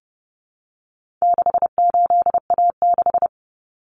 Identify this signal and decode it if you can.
Morse code, heard as a keyed tone, about 30 wpm, 700 Hz. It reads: 68A6
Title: Common Cut Numbers - 4-Digit 30wpm